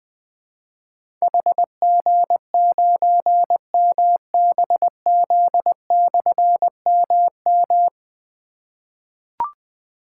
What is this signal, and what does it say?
Morse code, heard as a keyed tone, about 20 wpm, 700 Hz. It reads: HG9MBZ/MM
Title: Difficult Call Signs 20wpm